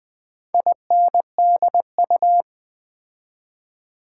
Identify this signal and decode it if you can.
Morse code, heard as a keyed tone, about 20 wpm, 700 Hz. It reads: INDU
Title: Build A Word - 4 Letters At A Time - 10-Letter Words - 20wpm